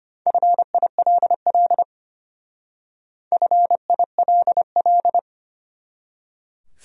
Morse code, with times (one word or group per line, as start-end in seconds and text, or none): 0.3-1.8 FILL
3.3-5.2 FILL